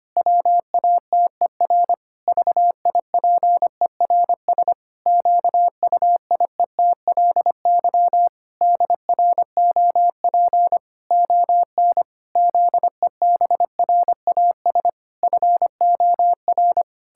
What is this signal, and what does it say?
Morse code, heard as a keyed tone, about 25 wpm, 700 Hz. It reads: WATER VIPERS QUIETLY DROP ON ZEBRAS FOR
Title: Pangram Challenge v2 - 25wpm